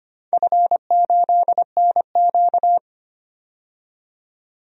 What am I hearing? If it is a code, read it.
Morse code, heard as a keyed tone, about 25 wpm, 700 Hz. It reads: F8NQ
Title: DXCC-100 Call Signs - 4 Characters - 25wpm